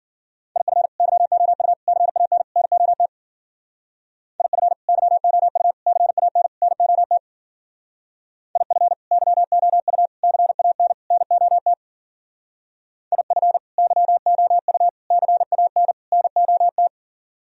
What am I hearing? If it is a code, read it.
Morse code, heard as a keyed tone, about 50 wpm, 700 Hz. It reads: IF YOU CAN NOT IF YOU CAN NOT IF YOU CAN NOT IF YOU CAN NOT
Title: Speed-Racing - Sets of 4 Words 40wpm